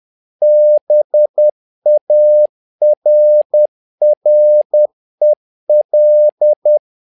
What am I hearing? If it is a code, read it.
Morse code, heard as a keyed tone, about 10 wpm, 600 Hz. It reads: BARREL
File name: learning center / tower trouble / tower trouble 10wpm